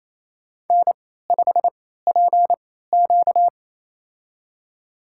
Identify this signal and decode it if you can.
Morse code, heard as a keyed tone, about 28 wpm, spaced as for 16 wpm, 700 Hz. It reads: N5PQ
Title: USA Call Signs - 4 Characters - 3x Character Spacing - 28wpm